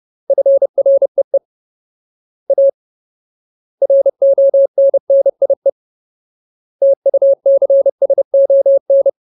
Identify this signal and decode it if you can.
Morse code, heard as a keyed tone, about 30 wpm, 550 Hz. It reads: FREE A RONNIE TUCSON
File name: Ultimate Word List / 30WPM Ultimate